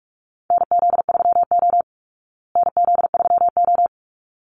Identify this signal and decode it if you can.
Morse code, heard as a keyed tone, about 44 wpm, 700 Hz. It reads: NZ3O NZ3O
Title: Call Signs Repeated 3x - 45wpm